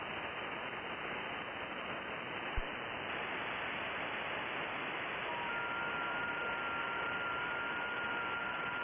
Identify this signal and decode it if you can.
FT8.
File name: contests / sept-vhf-k5tr- -2200